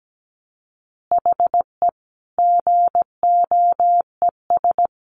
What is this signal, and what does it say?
Morse code, heard as a keyed tone, about 17 wpm, 700 Hz. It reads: HE GOES
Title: Additional Easy Phrases 17wpm